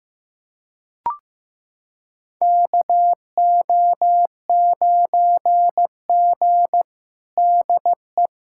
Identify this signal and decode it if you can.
Morse code, heard as a keyed tone, about 15 wpm, 700 Hz. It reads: KO9G DE
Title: USA Callsign DE Callsign - 4 Characters - 15wpm